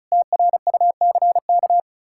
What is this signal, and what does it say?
Morse code, heard as a keyed tone, about 46 wpm, spaced as for 41 wpm, 700 Hz. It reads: TRUCK
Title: Speed-Racing Common QSO Phrases - 35wpm